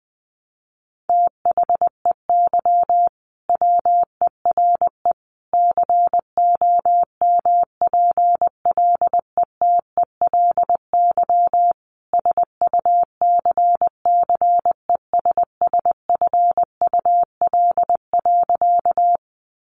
Morse code, heard as a keyed tone, about 19 wpm, 700 Hz. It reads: THEY WERE COMPLETELY SUCCESSFUL.
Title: Sentences from Top 1100 Words 20wpm